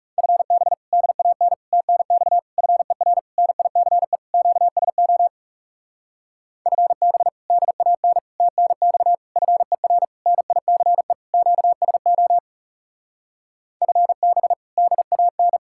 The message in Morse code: FB DAN TNX FER NICE QSO FB DAN TNX FER NICE QSO FB DAN

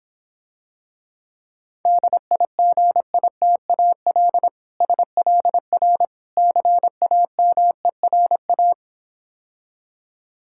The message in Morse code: DIGITAL SLR CAMERA

